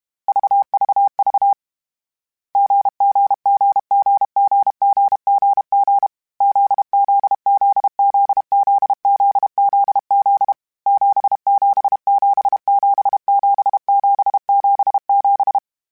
Morse code, heard as a keyed tone, about 32 wpm, 800 Hz. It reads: VVV GGGGGGGG ZZZZZZZZ 77777777
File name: Difficult Combos / Difficult Combos-mp3s / GZ GZ7 @30